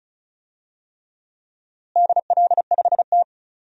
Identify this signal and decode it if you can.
Morse code, heard as a keyed tone, about 36 wpm, 700 Hz. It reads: DL5T
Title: DXCC-100 Call Signs - 4 Characters - Repeated 3x - 35wpm